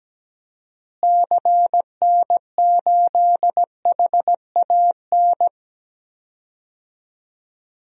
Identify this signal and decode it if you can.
Morse code, heard as a keyed tone, about 17 wpm, 700 Hz. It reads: CN8HAN
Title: DXCC-100 Call Signs - 6 Characters - Repeated 2x - 17wpm